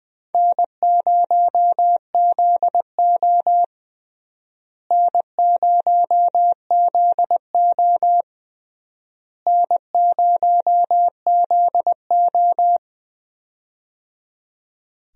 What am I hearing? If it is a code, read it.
Morse code, heard as a keyed tone, about 21 wpm, 700 Hz. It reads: N0ZO N0ZO N0ZO
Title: USA Call Signs - 4 Characters - Repeated 3x - 20wpm